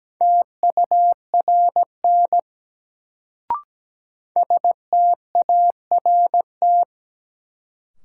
Morse code, heard as a keyed tone, about 17 wpm, 700 Hz. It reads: TURN START
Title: Sets of 4 Words - New Word Drill - 17wpm